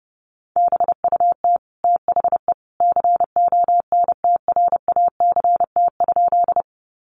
Morse code, heard as a keyed tone, about 30 wpm, 700 Hz. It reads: BUT THE CONTRACT?
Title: Sentences from Top 1400 Words 30wpm